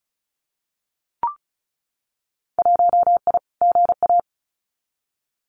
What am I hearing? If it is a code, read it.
Morse code, heard as a keyed tone, about 34 wpm, 700 Hz. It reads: 1I GA
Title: Winter Field Day Exchanges Encore 35wpm